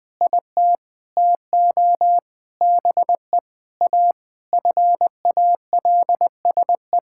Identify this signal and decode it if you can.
Morse code, heard as a keyed tone, about 20 wpm, 700 Hz. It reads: IT TO BE A FALSE
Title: Sentences from Top 1600 Words 20wpm